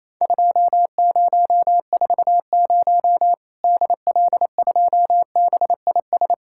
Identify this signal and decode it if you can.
Morse code, heard as a keyed tone, about 28 wpm, 700 Hz. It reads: 2040 DL2BIS
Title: Worked All Europe DX Contest - QTC Exchange - 28wpm